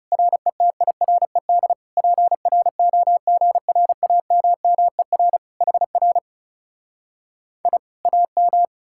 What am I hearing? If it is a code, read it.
Morse code, heard as a keyed tone, about 40 wpm, 700 Hz. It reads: RETIRED PROGRAMMER HR I AM